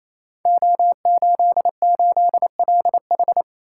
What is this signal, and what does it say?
Morse code, heard as a keyed tone, about 28 wpm, 700 Hz. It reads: O88LH
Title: Code Groups - 5 Letter:Numbers - 28wpm